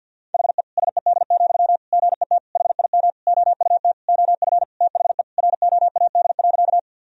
Morse code, heard as a keyed tone, about 49 wpm, 700 Hz. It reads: HE SED, GET HIM OUT OF THE ROAD.
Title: Sentences from Top 400 Words and Abbreviations 50wpm